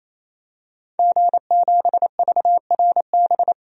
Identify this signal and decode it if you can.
Morse code, heard as a keyed tone, about 28 wpm, 700 Hz. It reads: G7VRB
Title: DXCC-100 Call Signs - 5 Characters - Repeated 3x - 28wpm